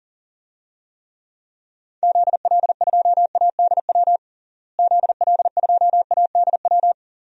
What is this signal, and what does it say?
Morse code, heard as a keyed tone, about 40 wpm, 700 Hz. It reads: ZL2ADW ZL2ADW
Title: DXCC-100 Call Signs - 6 Characters - Repeated 3x - 40wpm